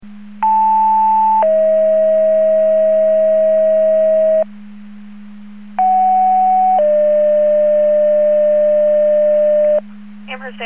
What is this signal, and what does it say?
Two-tone page, heard as a one-second tone, then a three-second tone, about 880 then 640 Hz, about 750 then 600 Hz, with speech. It is heard from 0.4 to 9.8 s.